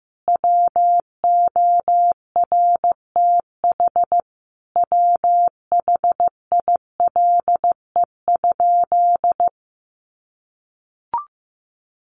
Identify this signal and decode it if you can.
Morse code, heard as a keyed tone, about 15 wpm, 700 Hz. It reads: WORTH WHILE?
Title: Sentences from Top 800 Words 15wpm